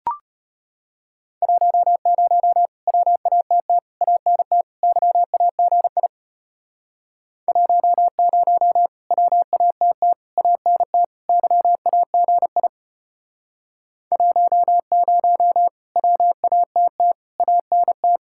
Morse code, heard as a keyed tone, about 31 wpm, 700 Hz. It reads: 10 WATT ANT YAGI 10 WATT ANT YAGI 10 WATT ANT
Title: Speed-Racing Common QSO Phrases - 25wpm